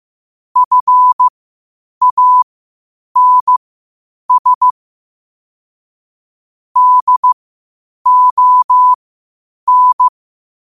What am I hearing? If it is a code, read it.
Morse code, heard as a keyed tone, about 15 wpm, spaced as for 8 wpm, 1000 Hz. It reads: FANS DON
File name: qotdfiles / 8wpm